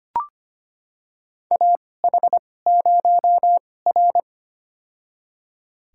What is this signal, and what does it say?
Morse code, heard as a keyed tone, about 25 wpm, spaced as for 18 wpm, 700 Hz. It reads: AH0R
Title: USA Call Signs - 4 Characters - 2x Character Spacing - 25wpm